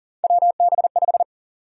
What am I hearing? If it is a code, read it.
Morse code, heard as a keyed tone, about 40 wpm, 700 Hz. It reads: WB5
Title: USA Call Sign Prefixes - 40wpm